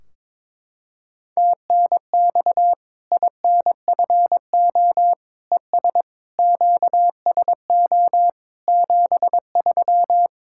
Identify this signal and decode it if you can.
Morse code, heard as a keyed tone, about 22 wpm, 700 Hz. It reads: TNX INFO ES QSO 73
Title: K1USN SST QSO Practice 22wpm